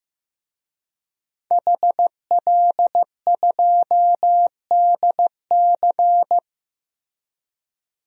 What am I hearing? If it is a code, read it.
Morse code, heard as a keyed tone, about 16 wpm, 700 Hz. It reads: HL2DC
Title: DXCC-100 Call Signs - 5 Characters - Repeated 2x - 15wpm